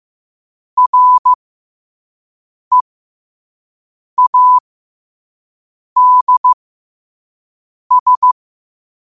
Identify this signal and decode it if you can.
Morse code, heard as a keyed tone, about 15 wpm, spaced as for 5 wpm, 1000 Hz. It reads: READS